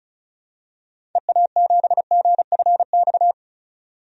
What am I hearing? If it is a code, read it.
Morse code, heard as a keyed tone, about 34 wpm, 700 Hz. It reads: EA7GFX
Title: DXCC-100 Call Signs - 6 Characters - 35wpm